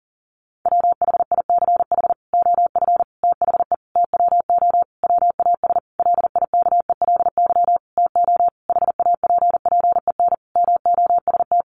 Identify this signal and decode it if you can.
Morse code, heard as a keyed tone, about 40 wpm, 700 Hz. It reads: WHICH OF THE TWO WAS LIKELY TO HAPPEN MOST
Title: Sentences from Top 1300 Words 40wpm